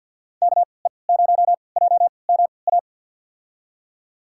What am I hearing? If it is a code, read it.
Morse code, heard as a keyed tone, about 53 wpm, spaced as for 29 wpm, 700 Hz. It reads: KE0JMA
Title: USA Call Signs - 3x Character Spacing - 50wpm